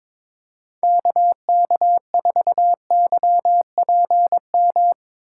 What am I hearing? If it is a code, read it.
Morse code, heard as a keyed tone, about 22 wpm, 700 Hz. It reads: KK4YPM
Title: USA Call Signs - 6 Characters - 3x Character Spacing - 22wpm